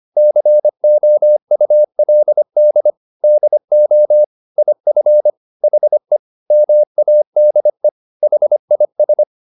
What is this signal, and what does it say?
Morse code, heard as a keyed tone, about 25 wpm, 600 Hz. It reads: COULD DO IF HE MADE HIS
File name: the old man and the sea part 1 25wpm